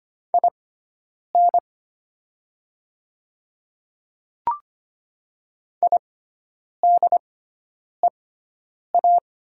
Morse code, heard as a keyed tone, about 25 wpm, spaced as for 9 wpm, 700 Hz. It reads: IN IDEA
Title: 6x Character Spacing - Top 300 Words - 25wpm